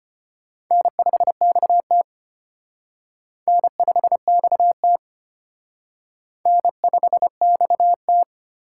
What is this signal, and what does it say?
Morse code, heard as a keyed tone, about 29 wpm, 700 Hz. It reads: N5XT N5XT N5XT
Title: Speed Racing - USA 1x2 Call Signs 25wpm